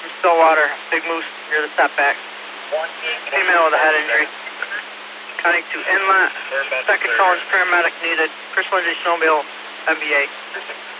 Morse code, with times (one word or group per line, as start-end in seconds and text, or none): none